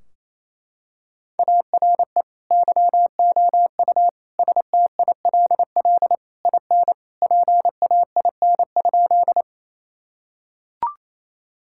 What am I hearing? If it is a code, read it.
Morse code, heard as a keyed tone, about 28 wpm, 700 Hz. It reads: ARE YOU STILL IN PAIN?
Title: Sentences from Top 1600 Words 28wpm